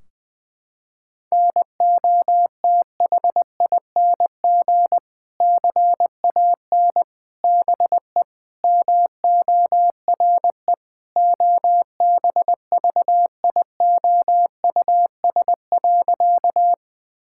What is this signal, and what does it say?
Morse code, heard as a keyed tone, about 20 wpm, 700 Hz. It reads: NOTHING CAN BE MORE OBVIOUS.
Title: Sentences from Top 1600 Words 20wpm